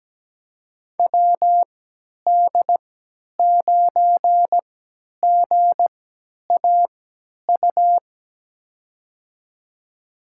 Morse code, heard as a keyed tone, about 17 wpm, spaced as for 10 wpm, 700 Hz. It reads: WD9GAU
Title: USA Call Signs - 6 Characters - 3x Character Spacing - 17wpm